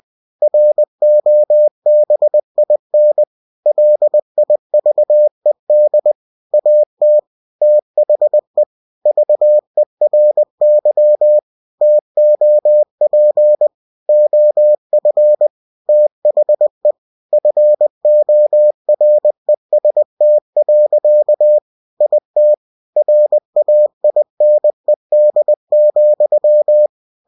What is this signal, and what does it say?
Morse code, heard as a keyed tone, about 20 wpm, 600 Hz. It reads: ROBIN LIVED AT THE VERY TOP OF THE FOREST. IT RAINED,
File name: cw-books / winnie-the-pooh / winnie-20wpm-page-0095